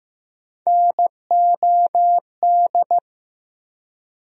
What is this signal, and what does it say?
Morse code, heard as a keyed tone, about 15 wpm, 700 Hz. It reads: NOD
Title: USA Call Sign Postfixes - 3 Letters 15wpm